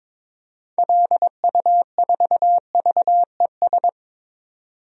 Morse code, heard as a keyed tone, about 23 wpm, 700 Hz. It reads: LU4VES
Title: DXCC-100 Call Signs - 6 Characters - 22wpm